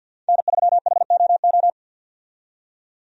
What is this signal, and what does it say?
Morse code, heard as a keyed tone, about 50 wpm, 700 Hz. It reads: N3HOO
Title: USA Call Signs - 5 Characters - 50wpm